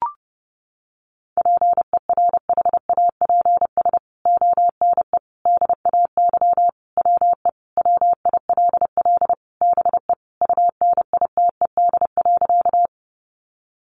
Morse code, heard as a keyed tone, about 30 wpm, 700 Hz. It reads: PERHAPS ONE DAY WE WILL BE UNITED.
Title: Sentences from Top 600 Words 30wpm